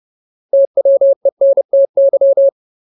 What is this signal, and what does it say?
Morse code, heard as a keyed tone, about 30 wpm, 550 Hz. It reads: TWENTY
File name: Ultimate Word List / 30WPM Ultimate